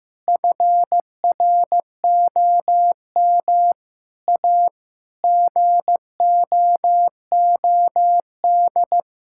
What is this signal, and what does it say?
Morse code, heard as a keyed tone, about 15 wpm, 700 Hz. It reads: FROM A GOOD
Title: Sentences from Top 1000 Words - Review Part 5 - 15wpm